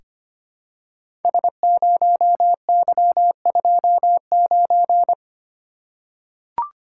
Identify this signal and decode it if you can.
Morse code, heard as a keyed tone, about 25 wpm, 700 Hz. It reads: S0Y29
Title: Code Groups - 5 Letter:Numbers - 25wpm